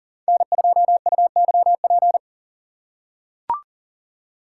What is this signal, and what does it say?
Morse code, heard as a keyed tone, about 40 wpm, 690 Hz. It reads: N2UYP
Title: USA Call Signs - 3x Character Spacing - 40wpm